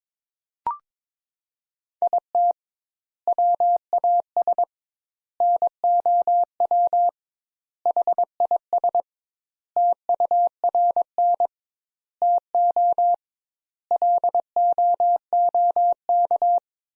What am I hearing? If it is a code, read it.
Morse code, heard as a keyed tone, about 22 wpm, 700 Hz. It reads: IT WAS NOW HIS TURN TO LOOK
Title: Sentences Top 200 Words - 22wpm